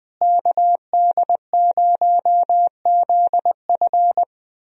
Morse code, heard as a keyed tone, about 20 wpm, 700 Hz. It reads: KD0ZF
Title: USA Call Signs - 5 Characters - Repeated 3x - 20wpm